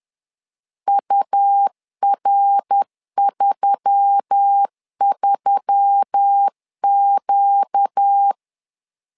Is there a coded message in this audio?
Yes: Morse code from 0.9 to 8.3 s, heard as a keyed tone, about 10 wpm, 800 Hz. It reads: UR33Q